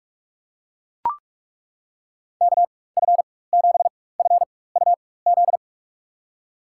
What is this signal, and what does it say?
Morse code, heard as a keyed tone, about 42 wpm, spaced as for 20 wpm, 700 Hz. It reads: KF7FUZ
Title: USA Call Signs - 6 Characters - 4x Character Spacing - 45wpm